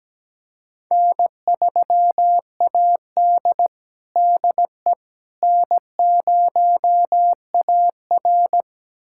Morse code, heard as a keyed tone, about 17 wpm, 700 Hz. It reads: N3AD DE N0AR
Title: USA Callsign DE Callsign - 4 Characters - 17wpm